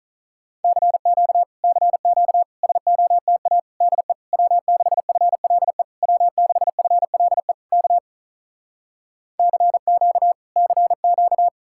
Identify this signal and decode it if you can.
Morse code, heard as a keyed tone, about 48 wpm, 700 Hz. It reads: CQ CQ SOTA DE W6FLE W6FLE K CQ CQ